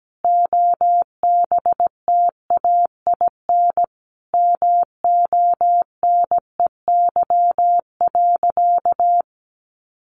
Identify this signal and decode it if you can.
Morse code, heard as a keyed tone, about 17 wpm, 700 Hz. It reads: OBTAIN MONEY.
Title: Sentences from Top 1300 Words 17wpm